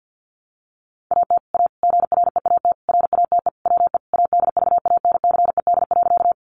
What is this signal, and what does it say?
Morse code, heard as a keyed tone, about 50 wpm, 700 Hz. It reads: AT A GREAT RATE WE ADVANCED.
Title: Sentences from Top 1200 Words 50wpm